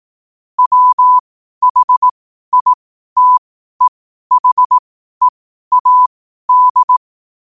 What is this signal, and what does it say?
Morse code, heard as a keyed tone, about 18 wpm, spaced as for 12 wpm, 1000 Hz. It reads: WHITEHEAD
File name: qotdfiles / 12wpm